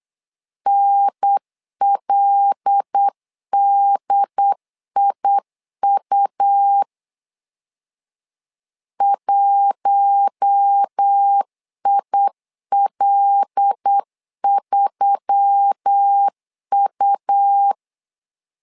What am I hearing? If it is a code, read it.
Morse code, heard as a keyed tone, about 8 wpm, 800 Hz. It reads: NLDIU 1IL3U
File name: LEZIONI / 5 LBVC / 5 L 8wpm 2831